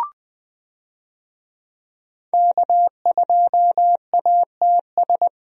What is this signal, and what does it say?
Morse code, heard as a keyed tone, about 20 wpm, 700 Hz. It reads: K2ATS
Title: USA Call Signs - 5 Characters - 20wpm